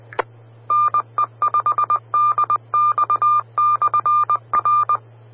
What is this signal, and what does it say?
Morse code, heard as a keyed tone, about 20 wpm, 1190 Hz. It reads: NE5DX/R
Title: Austin, TX Amateur Repeaters